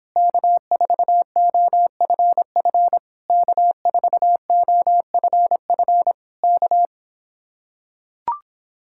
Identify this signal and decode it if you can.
Morse code, heard as a keyed tone, about 26 wpm, 700 Hz. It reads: K4OFF K4OFF K